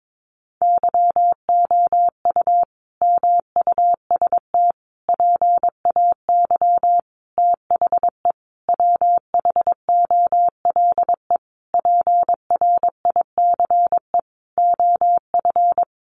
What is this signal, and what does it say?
Morse code, heard as a keyed tone, about 22 wpm, 700 Hz. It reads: YOU MUST PAY THE WHOLE PRICE OF
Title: Sentences from Top 1400 Words 22wpm